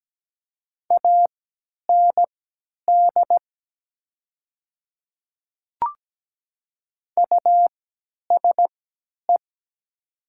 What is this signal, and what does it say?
Morse code, heard as a keyed tone, about 17 wpm, spaced as for 10 wpm, 700 Hz. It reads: AND USE
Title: 3x Character Spacing - Top 100 Words - 17wpm